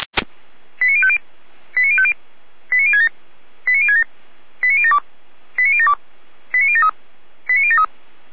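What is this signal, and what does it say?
Five-tone selective call (ZVEI), heard as quick runs of five tones, tones of about 70 ms. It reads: 89050 89050 89067 89067 89072 89072 89073 89073
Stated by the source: ZVEI five-tone selective call 89011 received by SDR.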